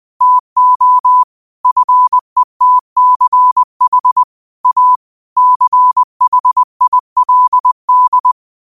Morse code, heard as a keyed tone, about 20 wpm, 1000 Hz. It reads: TO FETCH A CHILD
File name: qotdfiles / 20wpm